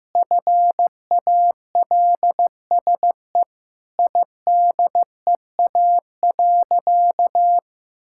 Morse code, heard as a keyed tone, about 15 wpm, 700 Hz. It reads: FALSE IDEA.
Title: Sentences from Top 1600 Words 15wpm